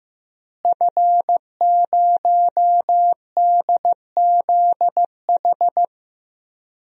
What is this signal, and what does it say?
Morse code, heard as a keyed tone, about 16 wpm, 700 Hz. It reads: F0DZH
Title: DXCC-100 Call Signs - 5 Characters - Repeated 2x - 15wpm